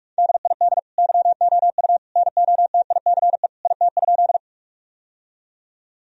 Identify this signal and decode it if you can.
Morse code, heard as a keyed tone, about 46 wpm, 700 Hz. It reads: DID YOU NOTICE IT?
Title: Sentences from Top 1000 Words - Review Part 2 - 45wpm